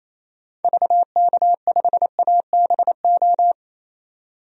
Morse code, heard as a keyed tone, about 27 wpm, 700 Hz. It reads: VK5ABO
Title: DXCC-100 Call Signs - 6 Characters - 28wpm